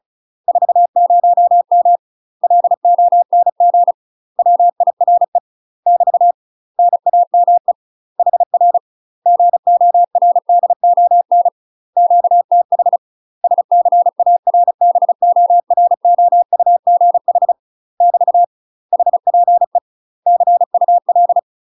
Morse code, heard as a keyed tone, about 34 wpm, 700 Hz. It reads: 40M LONG WIRE <BT> NAME HR GORDON QTH SCARBOROUGH <BT> HPE CUL
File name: UK MASTER QSOs TRAINING / 35WPM / UK MASTER QSOs 0004